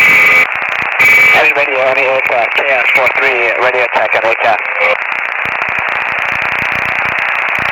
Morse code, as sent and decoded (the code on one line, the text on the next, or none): none